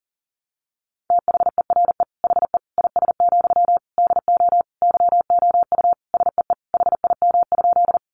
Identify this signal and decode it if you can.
Morse code, heard as a keyed tone, about 40 wpm, 700 Hz. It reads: THERE HE IS, DO YOU SEE HIM?
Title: Sentences from Top 100 Words 40wpm